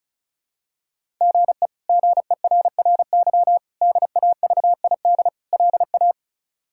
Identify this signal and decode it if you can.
Morse code, heard as a keyed tone, about 36 wpm, 700 Hz. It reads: GE GERRY DAVID LA
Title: K1USN SST QSO Practice - 35wpm